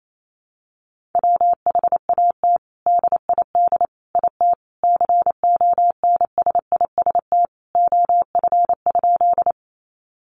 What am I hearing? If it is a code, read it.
Morse code, heard as a keyed tone, about 28 wpm, 700 Hz. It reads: WHAT DID IT CONSIST OF?
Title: Sentences from Top 1400 Words 28wpm